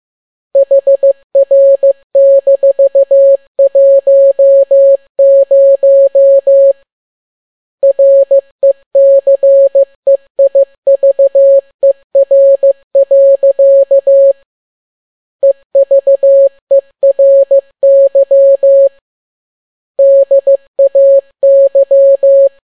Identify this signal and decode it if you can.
Morse code, heard as a keyed tone, about 15 wpm, 550 Hz. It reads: HR-10 RECEIVER. EVERY DAY
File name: n6tt p1 15C13